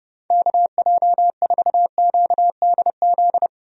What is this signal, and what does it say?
Morse code, heard as a keyed tone, about 30 wpm, 700 Hz. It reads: KJ4QDZ
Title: USA Call Signs - 6 Characters - 30wpm